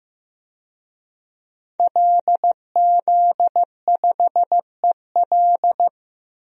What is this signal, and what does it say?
Morse code, heard as a keyed tone, about 15 wpm, 700 Hz. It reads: LZ5EL
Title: DXCC-100 Call Signs - 5 Characters - 15wpm